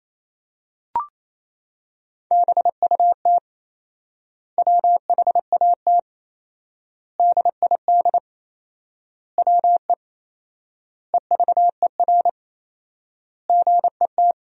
Morse code, heard as a keyed tone, about 28 wpm, 700 Hz. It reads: BUT WHAT DID WE EVER GET
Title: 4x Word-Spacing - Sentences from Top 400 Words - 28wpm